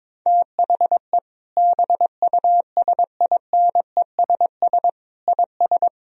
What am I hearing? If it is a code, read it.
Morse code, heard as a keyed tone, about 22 wpm, 700 Hz. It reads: THE BUSINESS IS
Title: Sentences from Top 1000 Words - Review Part 4 - 22wpm